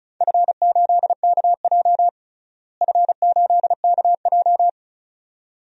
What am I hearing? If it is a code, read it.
Morse code, heard as a keyed tone, about 34 wpm, 700 Hz. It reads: F8KJ F8KJ
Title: DXCC-100 Call Signs - 4 Characters - Repeated 2x - 35wpm